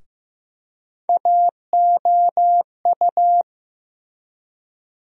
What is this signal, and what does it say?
Morse code, heard as a keyed tone, about 15 wpm, 700 Hz. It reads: AOU
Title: USA Call Sign Postfixes - 3 Letters 15wpm